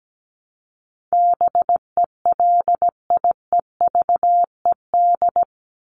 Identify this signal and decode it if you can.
Morse code, heard as a keyed tone, about 17 wpm, 700 Hz. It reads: BELIEVED
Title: ED Suffix - Top 100 Words with Base Words 17wpm